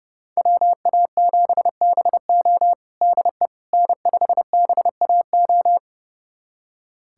Morse code, heard as a keyed tone, about 30 wpm, 700 Hz. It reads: WA7BO DE N5BAO
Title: USA Callsign DE Callsign - 5 Characters - 30wpm